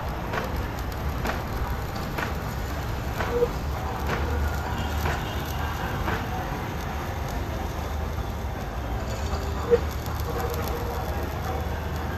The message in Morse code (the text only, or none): none